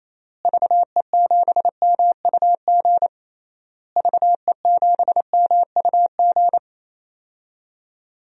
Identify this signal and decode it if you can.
Morse code, heard as a keyed tone, about 27 wpm, 700 Hz. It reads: VE7MUG VE7MUG
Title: DXCC-100 Call Signs - 6 Characters - Repeated 3x - 28wpm